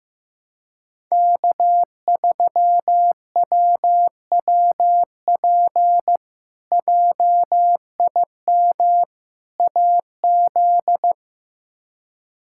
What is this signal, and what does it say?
Morse code, heard as a keyed tone, about 16 wpm, 700 Hz. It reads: K3WWP JIM AZ
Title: K1USN SST QSO Practice - 15wpm